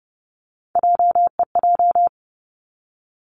Morse code, heard as a keyed tone, about 30 wpm, 700 Hz. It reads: JEJ
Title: Three Challenging Letter-Number Combinations 30wpm